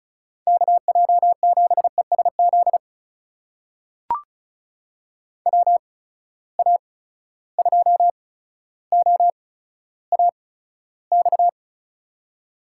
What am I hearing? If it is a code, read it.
Morse code, heard as a keyed tone, about 34 wpm, spaced as for 15 wpm, 700 Hz. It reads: KJ7ESZ WA2OAX
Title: USA Call Signs - 6 Characters - 8x Character Spacing - 35wpm